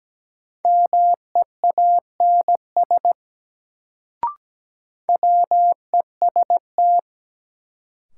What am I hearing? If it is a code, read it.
Morse code, heard as a keyed tone, about 17 wpm, 700 Hz. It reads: MEANS WEST
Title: Sets of 5 Words - New Word Drill - 17wpm